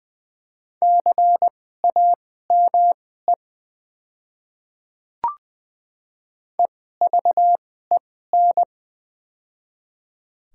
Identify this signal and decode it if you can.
Morse code, heard as a keyed tone, about 20 wpm, spaced as for 14 wpm, 700 Hz. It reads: CAME EVEN
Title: 2x Character Spacing - Top 200 Words - 20wpm